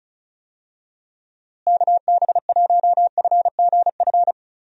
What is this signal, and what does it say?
Morse code, heard as a keyed tone, about 34 wpm, 700 Hz. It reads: KB1FGF
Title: USA Call Signs - 6 Characters - Repeated 2x - 35wpm